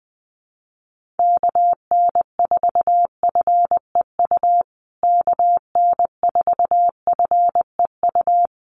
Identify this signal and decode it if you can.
Morse code, heard as a keyed tone, about 20 wpm, 700 Hz. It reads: KN4FEU KN4FEU
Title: USA Call Signs Repeated 2x 20wpm